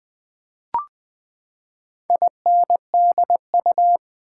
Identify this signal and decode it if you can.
Morse code, heard as a keyed tone, about 20 wpm, 700 Hz. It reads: INDU
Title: Build A Word - 4 Letters At A Time - 8-Letter Words - 20wpm